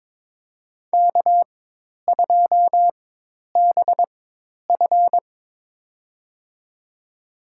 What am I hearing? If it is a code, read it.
Morse code, heard as a keyed tone, about 22 wpm, spaced as for 10 wpm, 700 Hz. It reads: K2BF
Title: USA Call Signs - 4 Characters - 4x Character Spacing - 22wpm